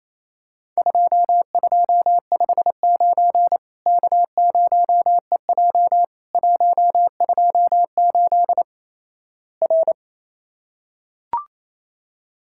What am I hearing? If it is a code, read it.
Morse code, heard as a keyed tone, about 28 wpm, 700 Hz. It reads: 2259 K0EJ 128 R
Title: Worked All Europe DX Contest - QTC Exchange - 28wpm